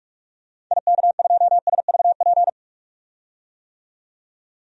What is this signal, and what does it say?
Morse code, heard as a keyed tone, about 44 wpm, 700 Hz. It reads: IK2SVP
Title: DXCC-100 Call Signs - 6 Characters - 45wpm